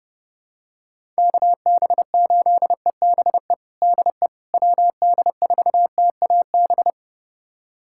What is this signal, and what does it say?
Morse code, heard as a keyed tone, about 30 wpm, 700 Hz. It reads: KB8EBE DE WD4TAB
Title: USA Callsign DE Callsign - 6 Characters - 30wpm